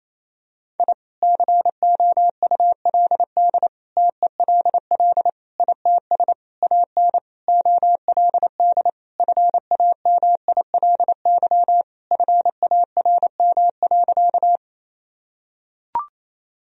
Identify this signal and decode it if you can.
Morse code, heard as a keyed tone, about 28 wpm, 700 Hz. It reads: I COULD TELL ITS AN OLD FAMILY FARM.
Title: Sentences from Top 1000 Words - Review Part 2 - 28wpm